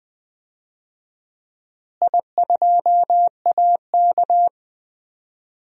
Morse code, heard as a keyed tone, about 20 wpm, 700 Hz. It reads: I2AK
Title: DXCC-100 Call Signs - 4 Characters - Repeated 3x - 20wpm